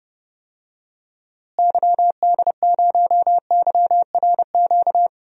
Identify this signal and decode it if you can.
Morse code, heard as a keyed tone, about 30 wpm, 700 Hz. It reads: YD0YRQ
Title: DXCC-100 Call Signs - 6 Characters - Repeated 3x - 30wpm